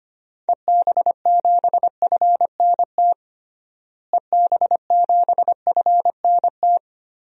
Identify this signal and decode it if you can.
Morse code, heard as a keyed tone, about 25 wpm, 700 Hz. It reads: EB7FNT EB7FNT
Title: DXCC-100 Call Signs - 6 Characters - Repeated 2x - 25wpm